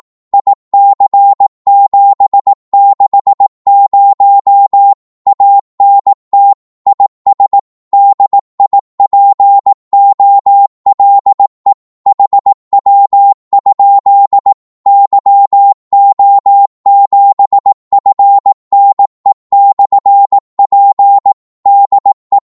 Morse code, heard as a keyed tone, about 18 wpm, 800 Hz. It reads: IC760 ANT IS DIPOLE HW? YO7FNE/P DE
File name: Morse / qso-18wpm